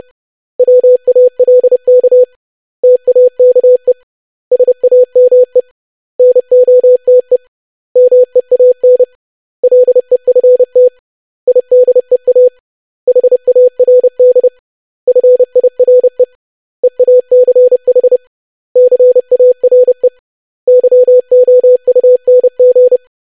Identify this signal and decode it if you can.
Morse code, heard as a keyed tone, about 30 wpm, 500 Hz. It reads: WALK TAKE SAME NOTE MEAN LEFT IDEA HARD FIRE EACH CARE YOUNG
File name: basic500-25wpm